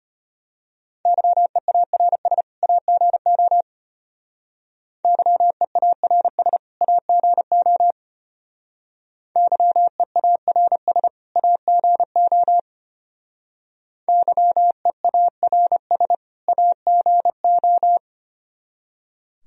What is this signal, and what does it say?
Morse code, heard as a keyed tone, about 32 wpm, 700 Hz. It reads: YEARS AGO YEARS AGO YEARS AGO YEARS AGO